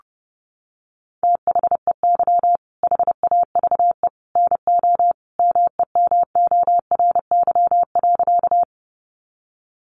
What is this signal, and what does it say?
Morse code, heard as a keyed tone, about 30 wpm, 700 Hz. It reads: THEY HAVE NO MEMORY.
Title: Sentences from Top 1300 Words 30wpm